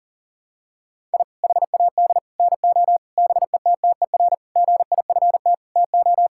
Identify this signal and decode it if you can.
Morse code, heard as a keyed tone, about 37 wpm, 700 Hz. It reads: I HAD NO BETTER GIFT TO
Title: Sentences from Top 1000 Words - Review Part 6 - 40wpm